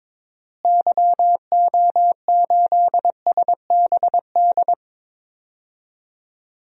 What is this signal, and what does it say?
Morse code, heard as a keyed tone, about 22 wpm, 700 Hz. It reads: YO8SBD
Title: DXCC-100 Call Signs - 6 Characters - Repeated 3x - 22wpm